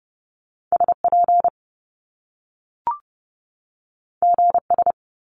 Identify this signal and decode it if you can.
Morse code, heard as a keyed tone, about 30 wpm, 700 Hz. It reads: SP GS
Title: Top Two Letter Combinations in Words 30wpm